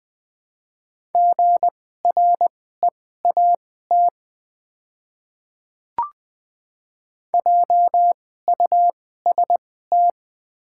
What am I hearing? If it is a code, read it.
Morse code, heard as a keyed tone, about 20 wpm, spaced as for 14 wpm, 700 Hz. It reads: GREAT JUST
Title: 2x Character Spacing - Top 300 Words - 20wpm